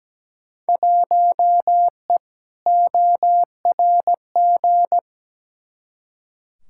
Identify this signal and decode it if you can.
Morse code, heard as a keyed tone, about 17 wpm, 700 Hz. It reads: 1E ORG
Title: Field Day Exchanges - 17wpm